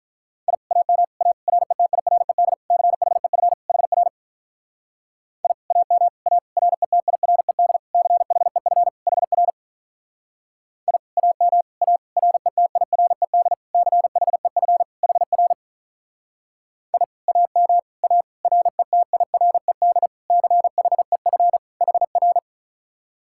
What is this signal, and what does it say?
Morse code, heard as a keyed tone, about 46 wpm, 700 Hz. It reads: I AM A RETIRED CHEF HR I AM A RETIRED CHEF HR I AM A RETIRED CHEF HR I AM A RETIRED CHEF HR